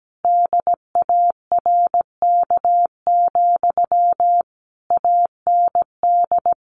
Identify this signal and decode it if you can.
Morse code, heard as a keyed tone, about 17 wpm, 700 Hz. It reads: DARK, AND
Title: Sentences from Top 1400 Words 17wpm